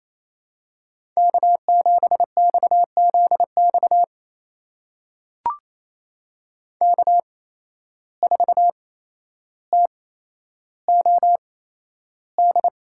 Morse code, heard as a keyed tone, about 28 wpm, spaced as for 12 wpm, 700 Hz. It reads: K7XZX K4TOD
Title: USA Call Signs - 5 Characters - 8x Character Spacing - 28wpm